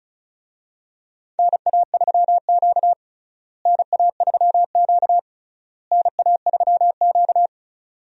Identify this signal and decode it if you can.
Morse code, heard as a keyed tone, about 36 wpm, 700 Hz. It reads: NA3Q NA3Q NA3Q
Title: USA Call Signs - 4 Characters - Repeated 3x - 35wpm